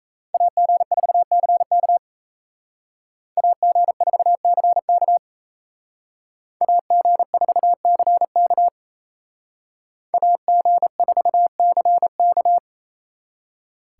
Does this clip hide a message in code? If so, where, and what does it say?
Yes: Morse code from 0.3 to 12.6 s, heard as a keyed tone, about 36 wpm, 700 Hz. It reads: AG4CK AG4CK AG4CK AG4CK